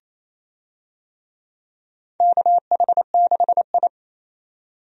Morse code, heard as a keyed tone, about 27 wpm, 700 Hz. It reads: KH6I
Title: DXCC-100 Call Signs - 4 Characters - 28wpm